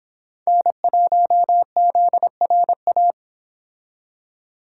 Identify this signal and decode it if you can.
Morse code, heard as a keyed tone, about 30 wpm, 700 Hz. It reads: N1ZRA